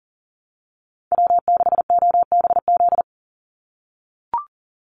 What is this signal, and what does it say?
Morse code, heard as a keyed tone, about 40 wpm, 700 Hz. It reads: W6OBZ
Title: USA Call Signs 40wpm